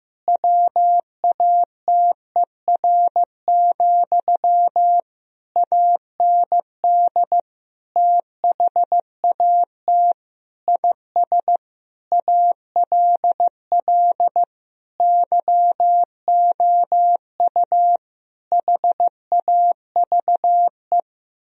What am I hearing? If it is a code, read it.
Morse code, heard as a keyed tone, about 15 wpm, 700 Hz. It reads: WATER, AND THAT IS ALL YOU HAVE
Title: Sentences from Top 1000 Words - Review Part 2 - 15wpm